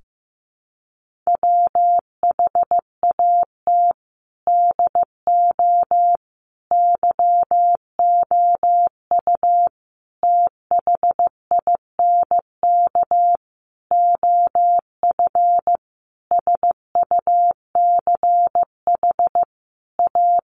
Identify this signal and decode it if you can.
Morse code, heard as a keyed tone, about 15 wpm, 700 Hz. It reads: WHAT DO YOU THINK OF SUCH A
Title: Sentences from Top 1400 Words 15wpm